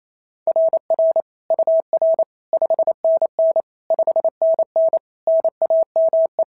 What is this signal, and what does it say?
Morse code, heard as a keyed tone, about 28 wpm, 690 Hz. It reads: RR UR 5NN 5NN NAME
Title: SOTA QSO Practice 28wpm